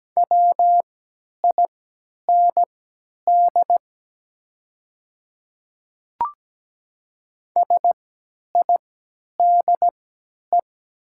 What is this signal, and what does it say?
Morse code, heard as a keyed tone, about 17 wpm, spaced as for 10 wpm, 700 Hz. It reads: WIND SIDE
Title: 3x Character Spacing - Top 500 Words - 17wpm